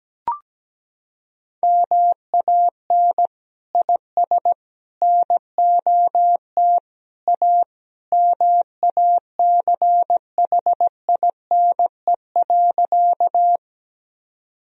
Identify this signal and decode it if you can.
Morse code, heard as a keyed tone, about 17 wpm, 700 Hz. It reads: MAN IS NOT A MACHINE.
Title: Sentences from Top 1000 Words - Review Part 2 - 17wpm